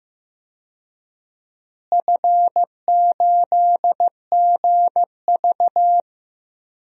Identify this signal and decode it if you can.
Morse code, heard as a keyed tone, about 15 wpm, 700 Hz. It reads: F8GV
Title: DXCC-100 Call Signs - 4 Characters - Repeated 2x - 15wpm